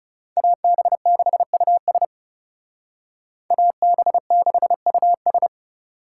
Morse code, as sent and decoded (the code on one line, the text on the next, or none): .- -... -.... ..- ... / .- -... -.... ..- ...
AB6US AB6US